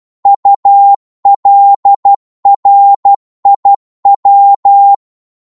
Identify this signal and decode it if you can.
Morse code, heard as a keyed tone, about 12 wpm, 800 Hz. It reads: ULRIW
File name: Morse / ltr-12wpm